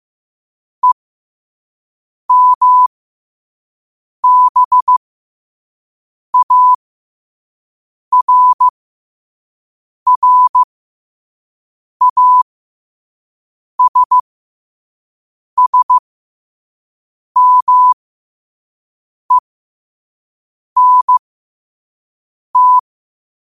Morse code, heard as a keyed tone, about 15 wpm, spaced as for 5 wpm, 1000 Hz. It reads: EMBARRASSMENT
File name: qotdfiles / 5wpm